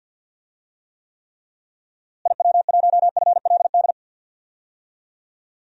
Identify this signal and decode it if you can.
Morse code, heard as a keyed tone, about 46 wpm, 700 Hz. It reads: IW1FLD
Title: DXCC-100 Call Signs - 6 Characters - Repeated 3x - 50wpm